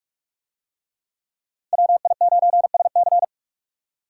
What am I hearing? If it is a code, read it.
Morse code, heard as a keyed tone, about 45 wpm, 700 Hz. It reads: WI9SC